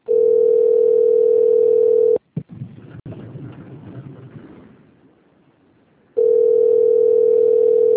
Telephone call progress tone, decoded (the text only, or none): ringback tone